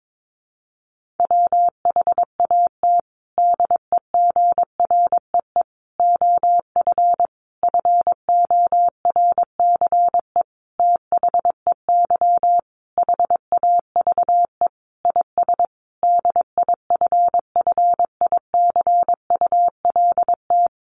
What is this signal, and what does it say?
Morse code, heard as a keyed tone, about 22 wpm, 700 Hz. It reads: WHAT DEGREE OF FORCE THEY HAVE IS DIFFICULT
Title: Sentences from Top 1500 Words 22wpm